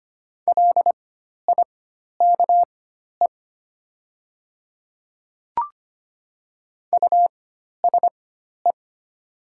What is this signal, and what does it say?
Morse code, heard as a keyed tone, about 25 wpm, spaced as for 12 wpm, 690 Hz. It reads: LIKE USE
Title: 4x Character Spacing - Top 200 Words - 25wpm